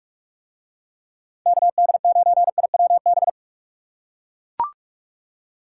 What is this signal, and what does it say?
Morse code, heard as a keyed tone, about 44 wpm, 700 Hz. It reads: KD9IWB